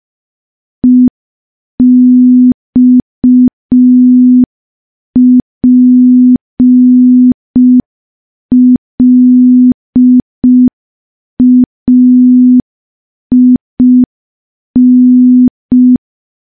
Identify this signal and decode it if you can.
Morse code, heard as a keyed tone, about 5 wpm, 250 Hz. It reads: EXPLAIN
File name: morse / morse-training-05WPM